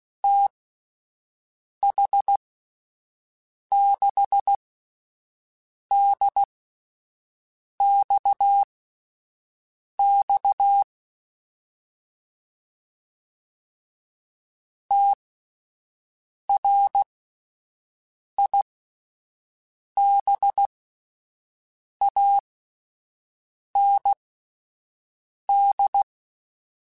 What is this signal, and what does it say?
Morse code, heard as a keyed tone, about 16 wpm, spaced as for 5 wpm, 800 Hz. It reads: TH6DXX TRIBAND
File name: QSOs / QSOs-mp3s / WD0BGP 1@13